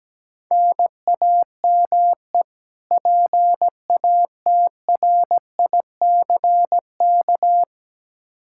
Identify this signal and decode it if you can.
Morse code, heard as a keyed tone, about 17 wpm, 700 Hz. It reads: NAME PATRICK
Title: K1USN SST QSO Practice 17wpm